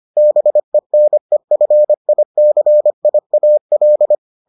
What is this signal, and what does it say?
Morse code, heard as a keyed tone, about 25 wpm, 600 Hz. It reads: BENEFICIAL